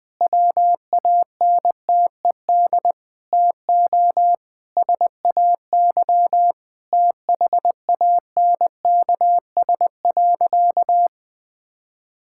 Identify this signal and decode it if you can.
Morse code, heard as a keyed tone, about 20 wpm, 700 Hz. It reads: WANTED TO SAY THANKS.
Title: Sentences with 4-Word Phrases - 20wpm